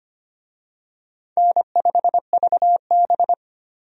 Morse code, heard as a keyed tone, about 25 wpm, 700 Hz. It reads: N5VB
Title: USA Call Signs - 4 Characters - 25wpm